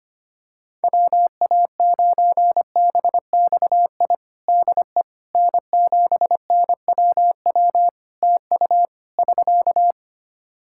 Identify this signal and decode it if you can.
Morse code, heard as a keyed tone, about 25 wpm, 700 Hz. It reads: WA9BXI DE N7NWW TU <SK>
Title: Practice QSO - Exchange 4 - Ending - 25wpm